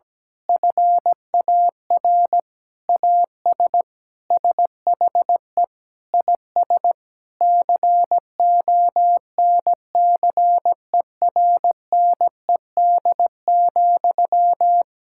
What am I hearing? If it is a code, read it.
Morse code, heard as a keyed tone, about 17 wpm, 700 Hz. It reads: FAR AS SHE IS CONCERNED,
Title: Sentences from Top 1000 Words - Review Part 6 - 17wpm